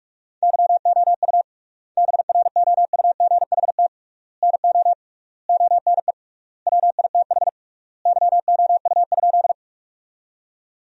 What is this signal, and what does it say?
Morse code, heard as a keyed tone, about 46 wpm, 700 Hz. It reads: YOU BROUGHT NO ONE WITH YOU?
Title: 3x Word-Spacing - Sentences from Top 500 Words - 45wpm